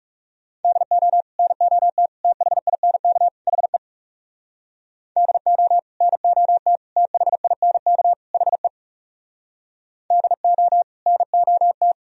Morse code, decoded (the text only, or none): DO NOT THINK HE DO NOT THINK HE DO NOT